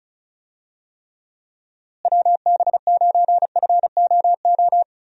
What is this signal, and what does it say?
Morse code, heard as a keyed tone, about 36 wpm, 700 Hz. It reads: WB9FOO
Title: USA Call Signs - 6 Characters - Repeated 3x - 35wpm